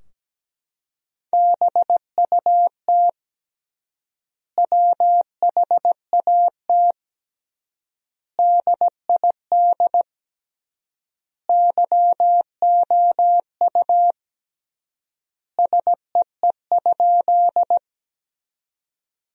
Morse code, heard as a keyed tone, about 17 wpm, 700 Hz. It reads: BUT WHAT DID YOU SEE?
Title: 3x Word-Spacing - Sentences from Top 100 Words - 17wpm